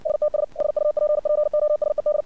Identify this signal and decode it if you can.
Morse code, heard as a keyed tone, about 56 wpm, 600 Hz. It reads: STS SUPPOSED
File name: archived / CRSnet